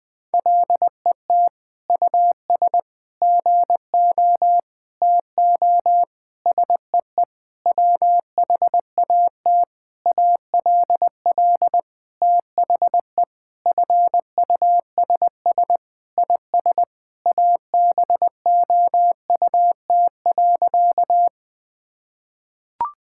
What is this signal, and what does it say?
Morse code, heard as a keyed tone, about 20 wpm, 700 Hz. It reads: LET US GO TO SEE WHAT ALL THE FUSS IS ABOUT.
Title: Sentences with 8-Word Phrases - 20wpm